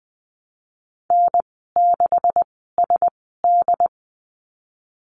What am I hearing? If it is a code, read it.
Morse code, heard as a keyed tone, about 20 wpm, spaced as for 15 wpm, 700 Hz. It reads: N6SD
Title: 167 Lesson 27g - Number 7 - Call Signs 20wpm